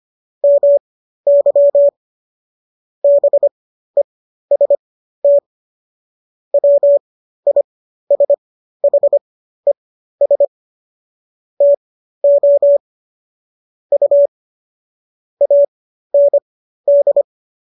Morse code, heard as a keyed tone, about 25 wpm, spaced as for 13 wpm, 570 Hz. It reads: MY BEST WISHES TO U AND